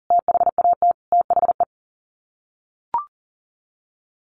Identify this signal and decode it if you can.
Morse code, heard as a keyed tone, about 40 wpm, 700 Hz. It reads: THAT THE
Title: Sets of 5 Words Encore 40wpm